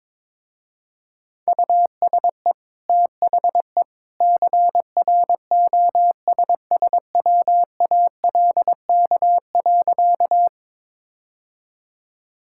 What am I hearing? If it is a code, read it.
Morse code, heard as a keyed tone, about 22 wpm, 700 Hz. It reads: USE THE CROSSWALK.
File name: practice / 22wpm / Easy Sentences with Hints - 22wpm